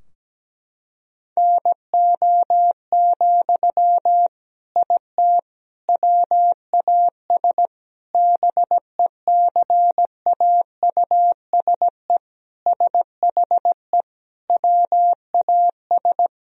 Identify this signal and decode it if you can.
Morse code, heard as a keyed tone, about 17 wpm, 700 Hz. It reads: NO, IT WAS BECAUSE SHE WAS
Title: Sentences from Top 1000 Words - Review Part 4 - 17wpm